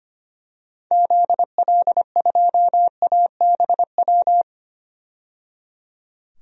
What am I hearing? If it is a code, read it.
Morse code, heard as a keyed tone, about 25 wpm, 700 Hz. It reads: ZL2ABW
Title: DXCC-100 Call Signs - 6 Characters - 25wpm